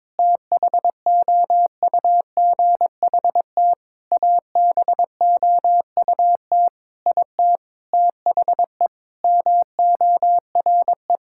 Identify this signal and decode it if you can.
Morse code, heard as a keyed tone, about 22 wpm, 700 Hz. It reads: THOUGHT ABOUT IT THE MORE
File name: practice / 22wpm / Sets of 8 Words - 22wpm